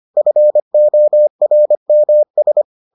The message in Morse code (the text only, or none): FORMS